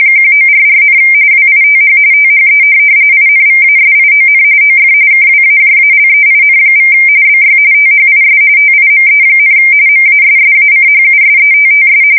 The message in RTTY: 1944, WITH THE INVASION OF KWAJALEIN= BY THIS TIME, MASSACHUSETTS

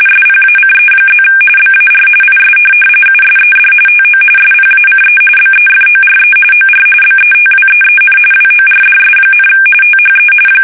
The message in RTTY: SPACE+ BAR IS USED TO SPACE BEFORE OR BETWEEN WORDS, THE

